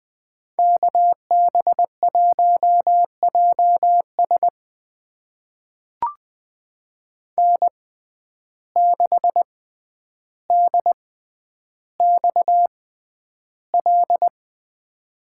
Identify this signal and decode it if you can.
Morse code, heard as a keyed tone, about 20 wpm, spaced as for 10 wpm, 700 Hz. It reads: KB1JS N6DXL
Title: USA Call Signs - 5 Characters - 6x Character Spacing - 20wpm